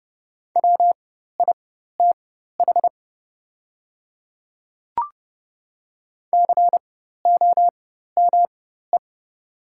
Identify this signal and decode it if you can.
Morse code, heard as a keyed tone, about 32 wpm, spaced as for 14 wpm, 700 Hz. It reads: WITH COME
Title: 4x Character Spacing - Top 100 Words - 30wpm